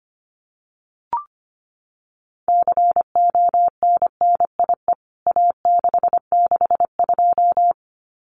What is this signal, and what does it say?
Morse code, heard as a keyed tone, about 25 wpm, 700 Hz. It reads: CONNIE A662
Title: CWT Exchange 25wpm